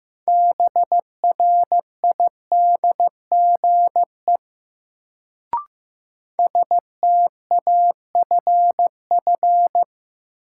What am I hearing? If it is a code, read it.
Morse code, heard as a keyed tone, about 15 wpm, 700 Hz. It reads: BRIDGE STAFF
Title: Sentences from Top 1200 Words - New Word Drill - 15wpm